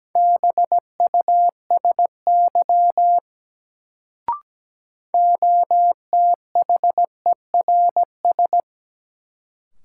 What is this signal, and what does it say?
Morse code, heard as a keyed tone, about 17 wpm, 700 Hz. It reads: BUSY OTHERS
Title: Sentences from Top 1500 Words - New Word Drill Alt - 17wpm